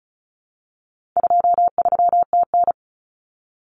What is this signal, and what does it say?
Morse code, heard as a keyed tone, about 34 wpm, 700 Hz. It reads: 23TN
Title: Common Cut Numbers - 4-Digit 35wpm